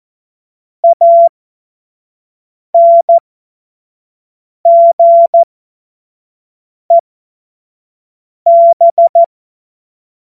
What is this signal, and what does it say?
Morse code, heard as a keyed tone, about 14 wpm, spaced as for 5 wpm, 680 Hz. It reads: ANGEB
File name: Morsekurs / Witze / Witz 25. Juli 2023 25 F70